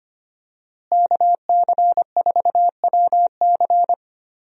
Morse code, heard as a keyed tone, about 25 wpm, 700 Hz. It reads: KC4WC
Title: USA Call Signs - 5 Characters - Repeated 3x - 25wpm